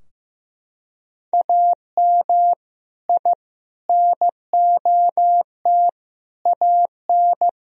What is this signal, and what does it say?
Morse code, heard as a keyed tone, about 15 wpm, 700 Hz. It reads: AM I NOT AN
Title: Sentences from Top 1000 Words - Review Part 3 - 15wpm